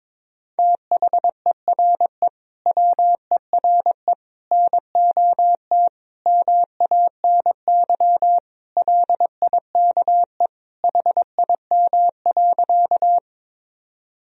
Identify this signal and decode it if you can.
Morse code, heard as a keyed tone, about 22 wpm, 700 Hz. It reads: THERE WERE NOT MANY LIKE HIM.
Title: Sentences from Top 1000 Words - Review Part 1 - 22wpm